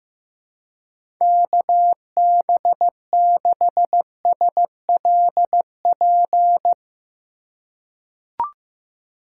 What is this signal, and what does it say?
Morse code, heard as a keyed tone, about 15 wpm, 700 Hz. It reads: KB6SLP
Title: USA Call Signs - 6 Characters - 3x Character Spacing - 15wpm